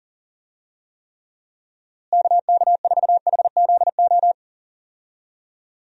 Morse code, heard as a keyed tone, about 37 wpm, 700 Hz. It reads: KK4HZO
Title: USA Call Signs - 6 Characters - Repeated 2x - 40wpm